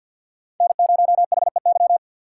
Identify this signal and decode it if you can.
Morse code, heard as a keyed tone, about 52 wpm, 700 Hz. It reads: N0HEY
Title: USA Call Signs - 5 Characters - 3x Character Spacing - 50wpm